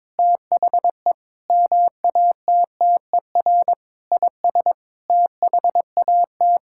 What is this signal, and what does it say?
Morse code, heard as a keyed tone, about 22 wpm, 700 Hz. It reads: THE MATTER IS THAT
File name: Sets of 9 Words - 22wpm